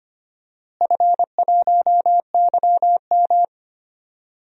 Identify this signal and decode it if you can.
Morse code, heard as a keyed tone, about 25 wpm, 700 Hz. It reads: F1YM
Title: DXCC-100 Call Signs - 4 Characters - Repeated 2x - 25wpm